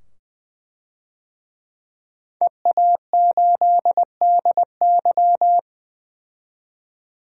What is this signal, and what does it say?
Morse code, heard as a keyed tone, about 20 wpm, 700 Hz. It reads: EA8DY
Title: DXCC-100 Call Signs - 5 Characters - 20wpm